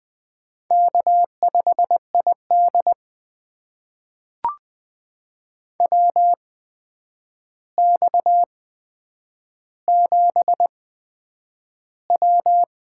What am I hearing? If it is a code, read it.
Morse code, heard as a keyed tone, about 20 wpm, spaced as for 9 wpm, 700 Hz. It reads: K5ID WX7W
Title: USA Call Signs - 4 Characters - 8x Character Spacing - 20wpm